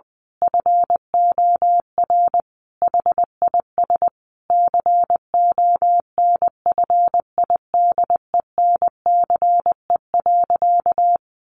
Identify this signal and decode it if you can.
Morse code, heard as a keyed tone, about 20 wpm, 700 Hz. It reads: FOR HIS CONFIDENCE.
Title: Sentences from Top 1500 Words 20wpm